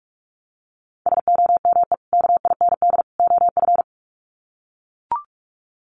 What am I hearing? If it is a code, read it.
Morse code, heard as a keyed tone, about 44 wpm, 700 Hz. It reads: SOME KIND OF
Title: Sets of 3 Words 45wpm